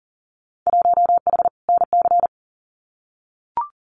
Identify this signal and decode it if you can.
Morse code, heard as a keyed tone, about 39 wpm, 700 Hz. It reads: 1H NC
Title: Winter Field Day Exchanges 40wpm